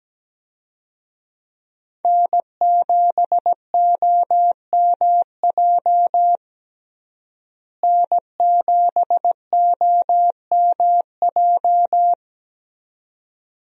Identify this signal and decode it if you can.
Morse code, heard as a keyed tone, about 17 wpm, 700 Hz. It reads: N7OMJ N7OMJ
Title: USA Call Signs - 5 Characters - Repeated 2x - 17wpm